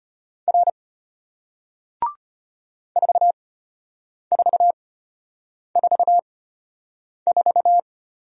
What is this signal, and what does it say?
Morse code, heard as a keyed tone, about 33 wpm, spaced as for 8 wpm, 700 Hz. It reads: R 4444